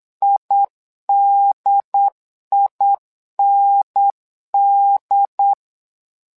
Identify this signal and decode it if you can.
Morse code, heard as a keyed tone, about 8 wpm, 800 Hz. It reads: IDIND
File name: LEZIONI / 2 UHDN / 2 UHDN 8wpm2232